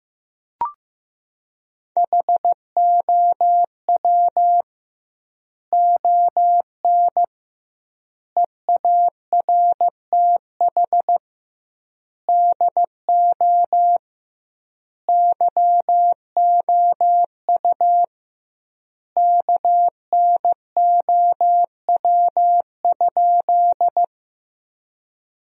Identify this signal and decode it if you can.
Morse code, heard as a keyed tone, about 15 wpm, 700 Hz. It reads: HOW ON EARTH DO YOU KNOW?
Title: 2x Word-Spacing - Sentences from Top 200 Words - 15wpm